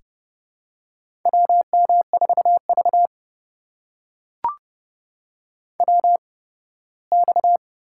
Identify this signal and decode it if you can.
Morse code, heard as a keyed tone, about 30 wpm, spaced as for 18 wpm, 700 Hz. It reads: WM4V WX
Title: USA Call Signs - 4 Characters - 8x Character Spacing - 30wpm